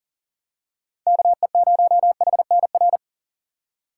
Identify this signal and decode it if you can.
Morse code, heard as a keyed tone, about 40 wpm, 700 Hz. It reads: KE0HNR
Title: USA Call Signs - 6x Character Spacing - 40wpm